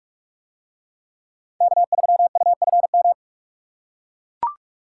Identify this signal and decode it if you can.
Morse code, heard as a keyed tone, about 43 wpm, 700 Hz. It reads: K3UFM